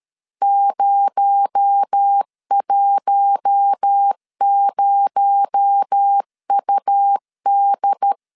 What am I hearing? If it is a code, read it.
Morse code, heard as a keyed tone, about 13 wpm, 800 Hz. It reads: 010UD
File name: LEZIONI / 6 PKX0 / 6 PKX0 12wpm 2231